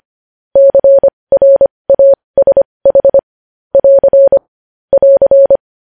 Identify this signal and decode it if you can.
Morse code, heard as a keyed tone, about 25 wpm, spaced as for 20 wpm, 550 Hz. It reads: CRASH <AR> <AR>
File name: podcast / News headlines-1x-25x20